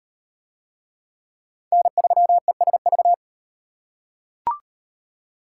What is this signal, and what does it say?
Morse code, heard as a keyed tone, about 33 wpm, 700 Hz. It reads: N3ESV